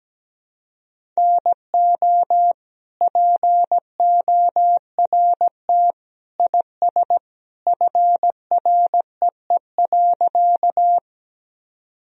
Morse code, heard as a keyed tone, about 17 wpm, 700 Hz. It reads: NO PORT IS FREE.
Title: Sentences from Top 1000 Words - Review Part 1 - 17wpm